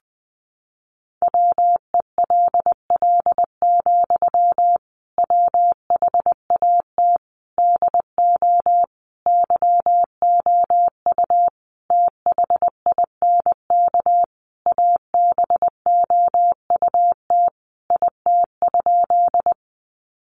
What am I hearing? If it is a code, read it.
Morse code, heard as a keyed tone, about 20 wpm, 700 Hz. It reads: WELL, WHAT DO YOU THINK ABOUT IT?
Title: Sentences from Top 200 Words 20wpm